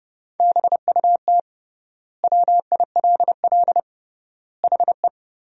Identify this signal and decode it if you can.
Morse code, heard as a keyed tone, about 32 wpm, 700 Hz. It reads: BUT WILL HE
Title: 3x Word-Spacing - Sentences from Top 100 Words - 30wpm